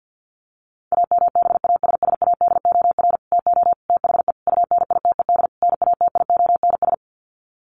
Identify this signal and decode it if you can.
Morse code, heard as a keyed tone, about 50 wpm, 700 Hz. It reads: AMBASSADOR TO THE UNITED NATIONS
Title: Sets of 5 Words Encore2 50wpm